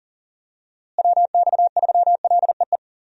Morse code, heard as a keyed tone, about 40 wpm, 700 Hz. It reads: WX3LEE
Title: USA Call Signs - 6 Characters - Repeated 2x - 40wpm